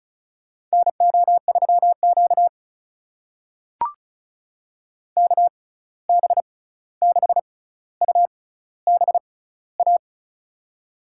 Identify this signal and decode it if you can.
Morse code, heard as a keyed tone, about 36 wpm, spaced as for 16 wpm, 700 Hz. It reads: NO3Q KB6UBA
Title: USA Call Signs - 6x Character Spacing - 35wpm